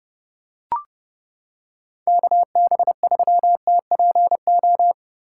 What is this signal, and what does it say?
Morse code, heard as a keyed tone, about 28 wpm, 700 Hz. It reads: KB3TPO